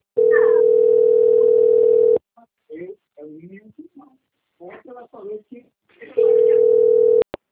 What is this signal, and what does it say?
Ringback tone (North America), 440 + 480 Hz, with speech.
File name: out- - - - - .30166